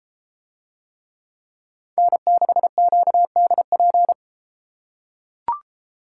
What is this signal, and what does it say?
Morse code, heard as a keyed tone, about 33 wpm, 700 Hz. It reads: N6QDP